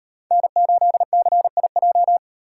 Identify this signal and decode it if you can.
Morse code, heard as a keyed tone, about 32 wpm, 700 Hz. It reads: N8CIJ